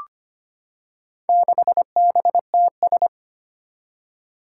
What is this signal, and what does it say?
Morse code, heard as a keyed tone, about 25 wpm, 700 Hz. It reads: 6BTS
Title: Four Challenging Letter-Number Combinations 25wpm